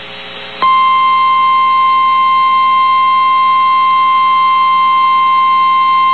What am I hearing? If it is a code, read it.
Emergency Alert System (SAME) tones (the 1050 Hz attention tone). It is heard from 1.0 to 5.0 s.
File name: eas received alerts / L1 19540 audio